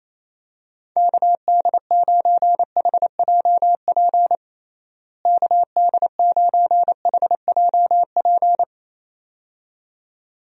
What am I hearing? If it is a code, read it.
Morse code, heard as a keyed tone, about 28 wpm, 700 Hz. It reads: KD9HJP KD9HJP
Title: USA Call Signs - 6 Characters - Repeated 2x - 28wpm